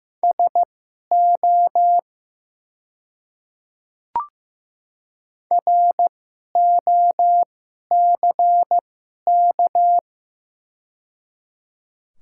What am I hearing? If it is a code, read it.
Morse code, heard as a keyed tone, about 15 wpm, spaced as for 11 wpm, 690 Hz. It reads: SO ROCK
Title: 2x Character Spacing - Top 500 Words - 15wpm